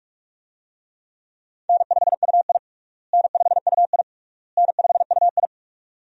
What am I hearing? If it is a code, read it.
Morse code, heard as a keyed tone, about 46 wpm, 700 Hz. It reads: N5UI N5UI N5UI
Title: USA Call Signs - 4 Characters - Repeated 3x - 45wpm